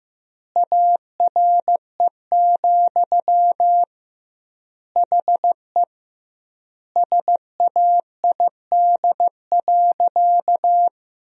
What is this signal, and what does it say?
Morse code, heard as a keyed tone, about 15 wpm, 700 Hz. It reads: ARE, HE SAID.
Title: 2x Word-Spacing - Sentences from Top 100 Words - 15wpm